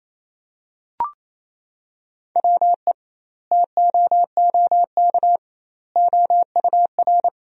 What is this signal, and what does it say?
Morse code, heard as a keyed tone, about 28 wpm, 700 Hz. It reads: WE TOOK OUR
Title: 2x Word-Spacing - Sentences from Top 400 Words - 28wpm